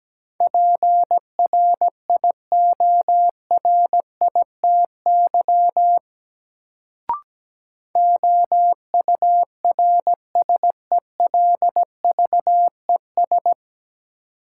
Morse code, heard as a keyed tone, about 17 wpm, 700 Hz. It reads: PRIORITY OURSELVES
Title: Sentences from Top 1600 Words - New Word Drill - 17wpm